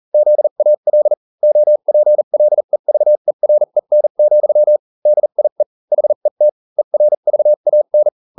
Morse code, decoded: ZAL OPLEVEREN, DIE HET ERVAN